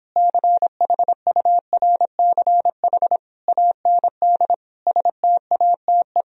Morse code, heard as a keyed tone, about 26 wpm, 700 Hz. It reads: CHURCH AND STATE